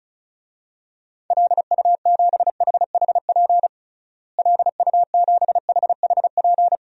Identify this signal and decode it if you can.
Morse code, heard as a keyed tone, about 36 wpm, 700 Hz. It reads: LU7HHP LU7HHP
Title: DXCC-100 Call Signs - 6 Characters - Repeated 2x - 35wpm